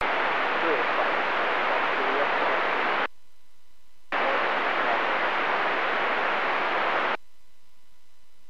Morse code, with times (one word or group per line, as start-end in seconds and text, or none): none